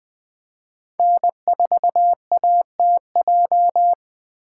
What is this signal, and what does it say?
Morse code, heard as a keyed tone, about 20 wpm, 700 Hz. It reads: N4ATJ
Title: 2023 POTA Activator Call Signs - 20wpm